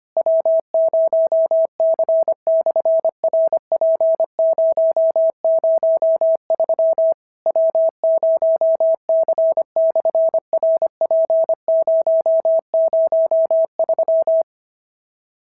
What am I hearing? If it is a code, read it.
Morse code, heard as a keyed tone, about 25 wpm, 690 Hz. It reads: W0C/RP003 W0C/RP003
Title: SOTA QSO Practice 25wpm